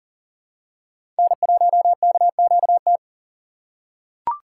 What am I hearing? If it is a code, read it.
Morse code, heard as a keyed tone, about 40 wpm, 700 Hz. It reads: N1KQT
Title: USA Call Signs - 5 Characters - 4x Character Spacing - 40wpm